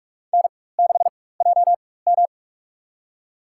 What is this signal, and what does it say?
Morse code, heard as a keyed tone, about 43 wpm, spaced as for 21 wpm, 700 Hz. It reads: N6JM
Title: USA Call Signs - 4 Characters - 4x Character Spacing - 45wpm